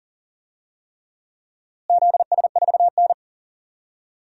Morse code, heard as a keyed tone, about 40 wpm, 700 Hz. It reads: ZS4N
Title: DXCC-100 Call Signs - 4 Characters - Repeated 3x - 40wpm